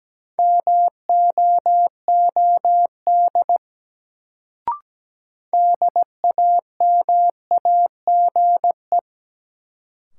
Morse code, heard as a keyed tone, about 17 wpm, 700 Hz. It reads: MOOD DAMAGE
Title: Sentences from Top 1100 Words - New Word Drill Alt - 17wpm